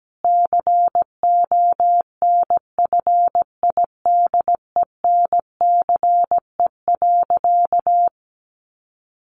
Morse code, heard as a keyed tone, about 17 wpm, 700 Hz. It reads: CONFIDENCE.
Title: Sentences from Top 1500 Words 17wpm